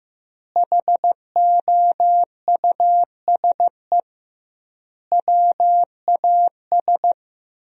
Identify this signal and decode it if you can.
Morse code, heard as a keyed tone, about 15 wpm, 700 Hz. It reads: HOUSE WAS
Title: 2x Word-Spacing - Sentences from Top 500 Words - 15wpm